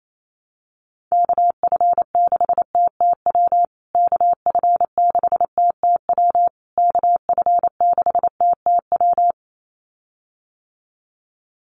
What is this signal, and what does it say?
Morse code, heard as a keyed tone, about 28 wpm, 700 Hz. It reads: KF6TTW KF6TTW KF6TTW
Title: USA Call Signs Repeated 3x 28wpm